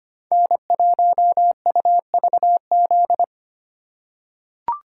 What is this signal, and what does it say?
Morse code, heard as a keyed tone, about 25 wpm, 700 Hz. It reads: N1UVZ
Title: USA Call Signs - 5 Characters - 2x Character Spacing - 25wpm